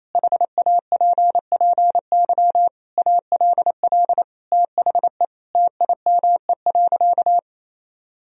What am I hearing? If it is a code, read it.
Morse code, heard as a keyed tone, about 28 wpm, 700 Hz. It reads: HAPPY ALL THE TIME.
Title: Sentences from Top 1000 Words - Review Part 4 - 28wpm